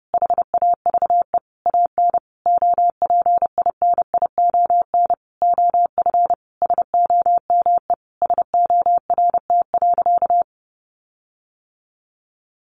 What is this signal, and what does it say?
Morse code, heard as a keyed tone, about 30 wpm, 700 Hz. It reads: HAVE AN OPINION OF SOME SORT.
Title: Sentences from Top 1300 Words 30wpm